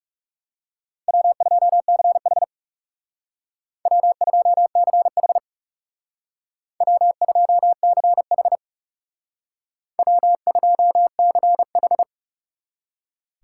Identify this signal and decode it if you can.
Morse code, heard as a keyed tone, about 39 wpm, 700 Hz. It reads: W2CH W2CH W2CH W2CH